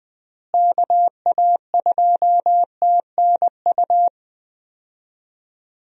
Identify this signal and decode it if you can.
Morse code, heard as a keyed tone, about 20 wpm, 700 Hz. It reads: KA2TNU
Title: USA Call Signs - 6 Characters - Repeated 2x - 20wpm